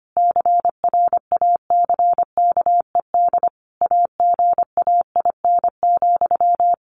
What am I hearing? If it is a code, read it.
Morse code, heard as a keyed tone, about 25 wpm, 700 Hz. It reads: CRACKED AGAIN,
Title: A Sound Of Thunder 25wpm